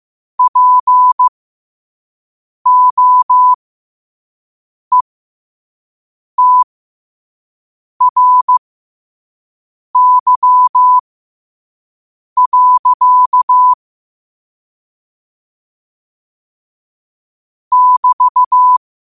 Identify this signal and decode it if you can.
Morse code, heard as a keyed tone, about 15 wpm, spaced as for 5 wpm, 1000 Hz. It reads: POETRY. <BT>
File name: qotdfiles / 5wpm